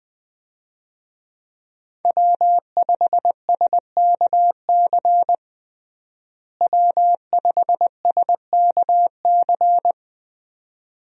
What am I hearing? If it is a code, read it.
Morse code, heard as a keyed tone, about 20 wpm, 700 Hz. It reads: W5SKC W5SKC
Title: USA Call Signs - 5 Characters - Repeated 2x - 20wpm